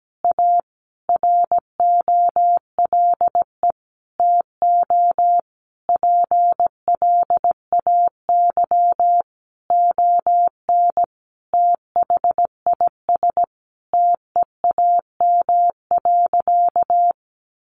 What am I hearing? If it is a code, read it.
Morse code, heard as a keyed tone, about 17 wpm, 700 Hz. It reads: A ROLE TO PLAY ON THIS TEAM.
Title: Sentences from Top 1500 Words 17wpm